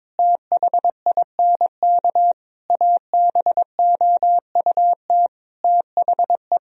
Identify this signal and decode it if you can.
Morse code, heard as a keyed tone, about 22 wpm, 700 Hz. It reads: THINK ABOUT THE
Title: Sentences from Top 1000 Words - Review Part 4 - 22wpm